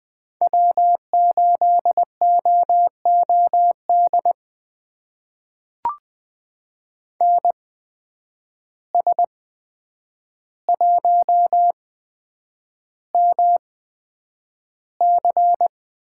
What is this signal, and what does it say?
Morse code, heard as a keyed tone, about 20 wpm, spaced as for 9 wpm, 700 Hz. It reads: W8OOD NS1MC
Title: USA Call Signs - 5 Characters - 8x Character Spacing - 20wpm